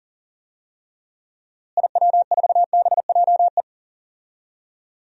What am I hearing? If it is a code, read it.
Morse code, heard as a keyed tone, about 39 wpm, 700 Hz. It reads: IW4BJE
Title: DXCC-100 Call Signs - 6 Characters - Repeated 2x - 40wpm